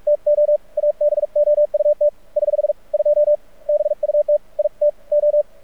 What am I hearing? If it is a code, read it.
Morse code, heard as a keyed tone, about 45 wpm, 590 Hz. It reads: TO ABOUT <HH> 2 BUT IT O
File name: archived / CRSnet